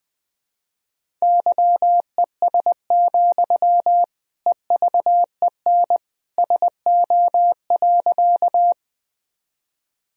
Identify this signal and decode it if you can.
Morse code, heard as a keyed tone, about 21 wpm, 700 Hz. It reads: YES, EVEN SO.
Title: Sentences from Top 1000 Words - Review Part 1 - 20wpm